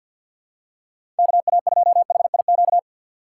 Morse code, heard as a keyed tone, about 53 wpm, 700 Hz. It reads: KA2HIQ
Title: USA Call Signs - 2x Character Spacing - 50wpm